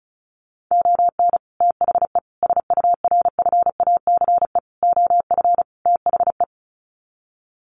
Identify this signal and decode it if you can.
Morse code, heard as a keyed tone, about 34 wpm, 700 Hz. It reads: ON THE SURFACE OF THE
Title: Sets of 5 Words Encore 35wpm